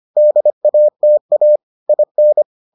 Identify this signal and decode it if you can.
Morse code, heard as a keyed tone, about 25 wpm, 600 Hz. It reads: DATA IN